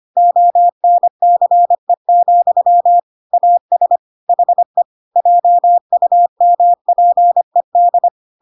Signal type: Morse code